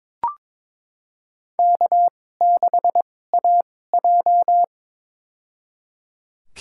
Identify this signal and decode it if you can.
Morse code, heard as a keyed tone, about 22 wpm, spaced as for 16 wpm, 700 Hz. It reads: K6AJ
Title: USA Call Signs - 4 Characters - 2x Character Spacing - 22wpm